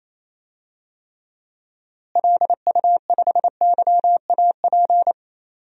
Morse code, heard as a keyed tone, about 28 wpm, 700 Hz. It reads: LU5YAP
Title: DXCC-100 Call Signs - 6 Characters - Repeated 3x - 28wpm